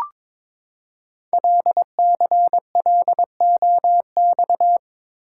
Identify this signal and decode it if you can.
Morse code, heard as a keyed tone, about 23 wpm, 700 Hz. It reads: LCLOX
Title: Code Groups - 5 Letter:Numbers - 22wpm